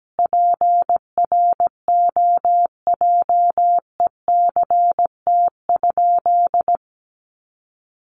Morse code, heard as a keyed tone, about 17 wpm, 700 Hz. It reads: PROJECT?
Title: Sentences from Top 1400 Words 17wpm